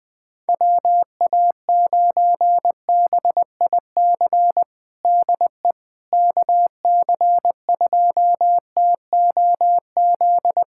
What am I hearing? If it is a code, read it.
Morse code, heard as a keyed tone, about 20 wpm, 700 Hz. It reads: WA9BIC DE KC2TOZ
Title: USA Callsign DE Callsign 20wpm